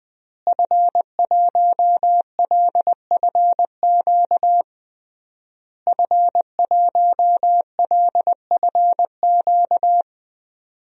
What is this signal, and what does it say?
Morse code, heard as a keyed tone, about 20 wpm, 700 Hz. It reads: F1LFQ F1LFQ
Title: DXCC-100 Call Signs - 5 Characters - Repeated 3x - 20wpm